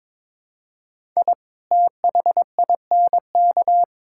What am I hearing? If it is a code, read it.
Morse code, heard as a keyed tone, about 22 wpm, 700 Hz. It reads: I THINK
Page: morse-code-ninja-practice-22wpm